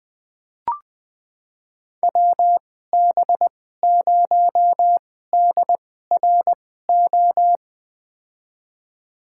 Morse code, heard as a keyed tone, about 20 wpm, spaced as for 14 wpm, 700 Hz. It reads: WB0DRO
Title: USA Call Signs - 6 Characters - 2x Character Spacing - 20wpm